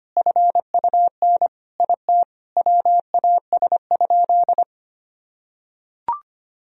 Morse code, heard as a keyed tone, about 25 wpm, 700 Hz. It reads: FUN IT WAS?
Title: Sentences from Top 1000 Words - Review Part 4 - 25wpm